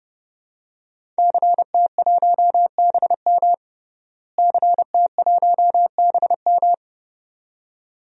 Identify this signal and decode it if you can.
Morse code, heard as a keyed tone, about 30 wpm, 700 Hz. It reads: CT1BM CT1BM
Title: DXCC-100 Call Signs - 5 Characters - Repeated 2x - 30wpm